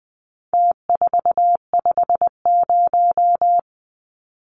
Morse code, heard as a keyed tone, about 20 wpm, 700 Hz. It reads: T450
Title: Numbers - Contest Style - 4-Digit 20wpm